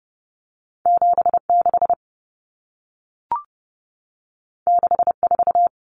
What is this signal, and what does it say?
Morse code, heard as a keyed tone, about 30 wpm, 700 Hz. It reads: 76 64
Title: Numbers - 2-Digit 30wpm